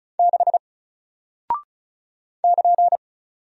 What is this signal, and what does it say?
Morse code, heard as a keyed tone, about 36 wpm, 700 Hz. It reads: 6 <KN>
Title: Single Character (Letter:Number:Punctuation:Prosign) Alt 35wpm